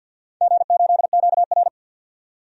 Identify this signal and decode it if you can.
Morse code, heard as a keyed tone, about 51 wpm, 700 Hz. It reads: G8QR
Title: DXCC-100 Call Signs - 4 Characters - 50wpm